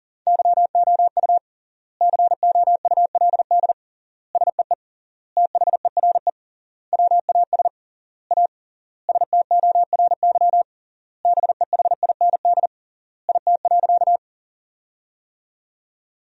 Morse code, heard as a keyed tone, about 40 wpm, 700 Hz. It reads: YOU COULD SEE THERE WAS A STORY BEHIND IT.
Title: 3x Word-Spacing - Sentences from Top 500 Words - 40wpm